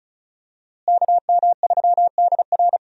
Morse code, heard as a keyed tone, about 36 wpm, 700 Hz. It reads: KM3DR
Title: USA Call Signs - 5 Characters - Repeated 3x - 35wpm